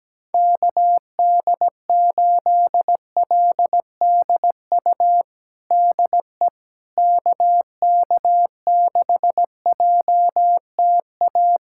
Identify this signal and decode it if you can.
Morse code, heard as a keyed tone, about 17 wpm, 700 Hz. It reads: KD8LDU DE KK6JTA
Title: USA Callsign DE Callsign - 6 Characters - 17wpm